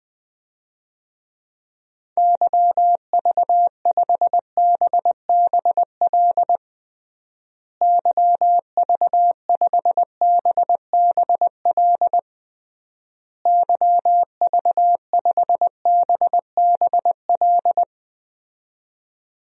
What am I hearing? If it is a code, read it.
Morse code, heard as a keyed tone, about 20 wpm, 700 Hz. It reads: YV5BBL YV5BBL YV5BBL
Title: DXCC-100 Call Signs - 6 Characters - Repeated 3x - 20wpm